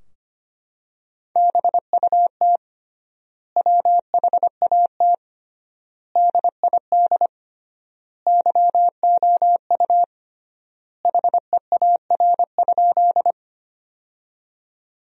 Morse code, heard as a keyed tone, about 25 wpm, 700 Hz. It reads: BUT WHAT DID YOU HEAR?
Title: 3x Word-Spacing - Sentences from Top 300 Words - 25wpm